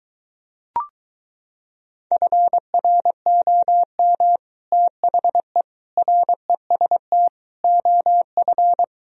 Morse code, heard as a keyed tone, about 23 wpm, 700 Hz. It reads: FROM THE REST OF